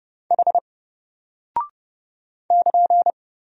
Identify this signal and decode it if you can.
Morse code, heard as a keyed tone, about 30 wpm, 700 Hz. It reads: H <KN>
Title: Single Character (Letter:Number:Punctuation:Prosign) Alt 30wpm